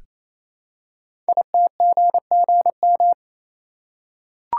Morse code, heard as a keyed tone, about 28 wpm, 700 Hz. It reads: ITGGM
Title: Code Groups - 5 Letters - 28wpm